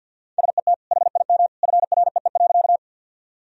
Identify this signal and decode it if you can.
Morse code, heard as a keyed tone, about 50 wpm, 700 Hz. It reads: SET HIM FREE.
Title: Sentences from Top 1000 Words - Review Part 1 - 50wpm